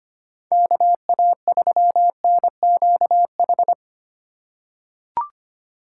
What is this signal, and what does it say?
Morse code, heard as a keyed tone, about 25 wpm, 700 Hz. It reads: KA3NQH
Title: USA Call Signs - 6 Characters - 4x Character Spacing - 25wpm